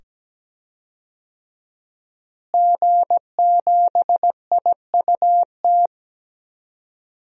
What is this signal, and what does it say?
Morse code, heard as a keyed tone, about 17 wpm, 700 Hz. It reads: G7IUT
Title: DXCC-100 Call Signs - 5 Characters - Repeated 2x - 17wpm